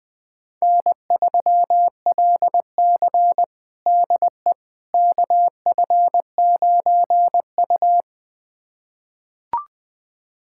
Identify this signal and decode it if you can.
Morse code, heard as a keyed tone, about 20 wpm, 700 Hz. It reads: N3LC DE KF9U
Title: USA Callsign DE Callsign - 4 Characters - 20wpm